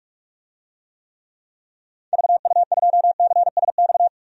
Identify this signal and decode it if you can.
Morse code, heard as a keyed tone, about 46 wpm, 700 Hz. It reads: VU2CSX
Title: DXCC-100 Call Signs - 6 Characters - 45wpm